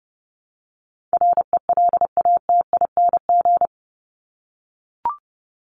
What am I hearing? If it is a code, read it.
Morse code, heard as a keyed tone, about 30 wpm, 700 Hz. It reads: RELATING
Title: ING Suffix - Top 100 Words with Base Words 30wpm